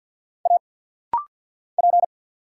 Morse code, heard as a keyed tone, about 49 wpm, 700 Hz. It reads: A P
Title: Single Letter-Number - Rapid-Fire 50wpm